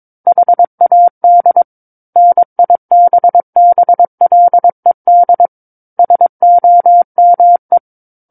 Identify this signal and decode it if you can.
Morse code, heard as a keyed tone, about 22 wpm, 700 Hz. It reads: HAD NIBBLED SOME
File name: 06-Chapter-06 22 WPM